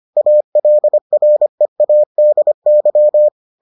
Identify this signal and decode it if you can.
Morse code, heard as a keyed tone, about 25 wpm, 600 Hz. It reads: ALREADY